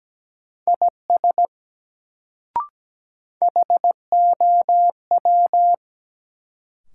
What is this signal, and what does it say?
Morse code, heard as a keyed tone, about 17 wpm, 700 Hz. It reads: IS HOW
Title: Sets of 2 Words - New Word Drill - 17wpm